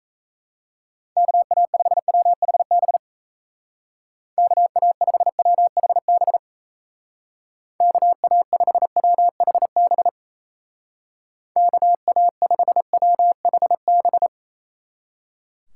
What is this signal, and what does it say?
Morse code, heard as a keyed tone, about 35 wpm, 700 Hz. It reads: KA5WHB KA5WHB KA5WHB KA5WHB
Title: Speed Racing - USA 2x3 Call Signs 28wpm